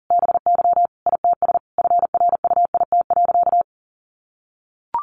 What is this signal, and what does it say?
Morse code, heard as a keyed tone, about 40 wpm, 700 Hz. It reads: BY ITS FRUIT.
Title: Sentences from Top 1300 Words 40wpm